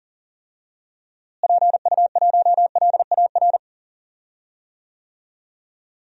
Morse code, heard as a keyed tone, about 40 wpm, 700 Hz. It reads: PU1LAR
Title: DXCC-100 Call Signs - 6 Characters - Repeated 3x - 40wpm